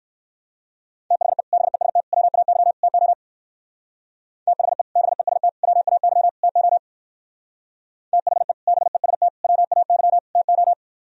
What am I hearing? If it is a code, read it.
Morse code, heard as a keyed tone, about 59 wpm, 700 Hz. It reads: THE BEST WAY TO THE BEST WAY TO THE BEST WAY TO